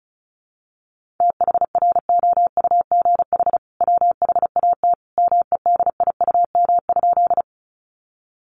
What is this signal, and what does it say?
Morse code, heard as a keyed tone, about 34 wpm, 700 Hz. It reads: THROUGH WHAT MEDIUM?
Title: Sentences from Top 900 Words 35wpm